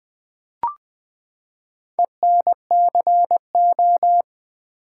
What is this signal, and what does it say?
Morse code, heard as a keyed tone, about 20 wpm, 700 Hz. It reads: ENCO
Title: Build A Word - 4 Letters At A Time - 11-Letter Words - 20wpm